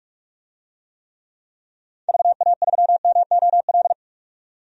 Morse code, heard as a keyed tone, about 46 wpm, 700 Hz. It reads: VA3MOL
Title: DXCC-100 Call Signs - 6 Characters - 45wpm